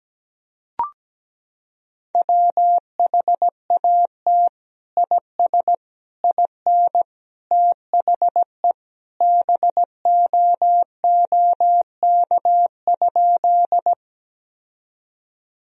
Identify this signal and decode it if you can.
Morse code, heard as a keyed tone, about 17 wpm, 700 Hz. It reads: WHAT IS IN THE BOOK?
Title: Sentences from Top 1000 Words - Review Part 1 - 17wpm